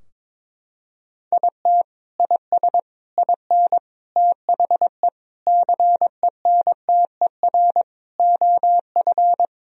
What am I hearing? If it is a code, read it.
Morse code, heard as a keyed tone, about 21 wpm, 700 Hz. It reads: IT IS IN THE CENTER OF
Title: Sentences from Top 1000 Words - Review Part 2 - 22wpm